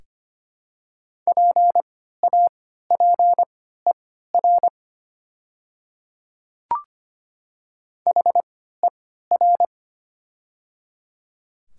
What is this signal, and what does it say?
Morse code, heard as a keyed tone, about 25 wpm, spaced as for 14 wpm, 700 Hz. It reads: PAPER HER
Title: 3x Character Spacing - Top 500 Words - 25wpm